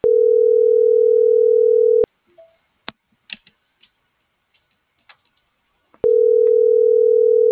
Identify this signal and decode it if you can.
Ringback tone (North America), 440 + 480 Hz.